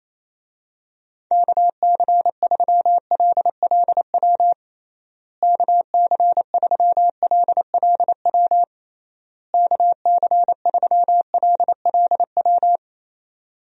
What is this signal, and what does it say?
Morse code, heard as a keyed tone, about 27 wpm, 700 Hz. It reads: KC3LLW KC3LLW KC3LLW
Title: USA Call Signs - 6 Characters - Repeated 3x - 28wpm